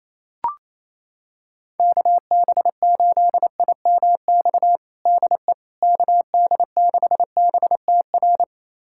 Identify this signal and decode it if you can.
Morse code, heard as a keyed tone, about 27 wpm, 700 Hz. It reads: KB8IMX DE KD6BTR
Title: USA Callsign DE Callsign 28wpm